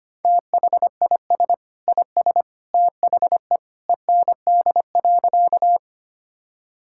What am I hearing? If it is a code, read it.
Morse code, heard as a keyed tone, about 25 wpm, 700 Hz. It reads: THIS IS THE END.
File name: practice / 25wpm / Sentences with 2-Word Phrases - 25wpm